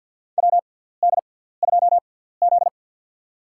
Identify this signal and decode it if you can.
Morse code, heard as a keyed tone, about 53 wpm, spaced as for 17 wpm, 700 Hz. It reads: WD2Z
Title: USA Call Signs - 4 Characters - 6x Character Spacing - 50wpm